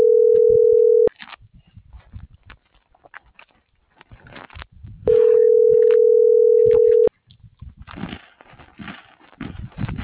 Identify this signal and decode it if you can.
Ringback tone (North America), 440 + 480 Hz.